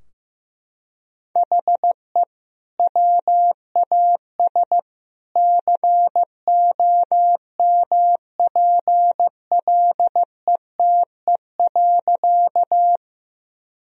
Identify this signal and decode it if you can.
Morse code, heard as a keyed tone, about 15 wpm, 700 Hz. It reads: HE WAS COMPLETE.
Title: Sentences from Top 1000 Words - Review Part 2 - 15wpm